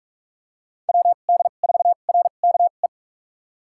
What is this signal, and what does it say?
Morse code, heard as a keyed tone, about 47 wpm, spaced as for 33 wpm, 700 Hz. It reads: WD4RKE
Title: USA Call Signs - 6 Characters - 2x Character Spacing - 45wpm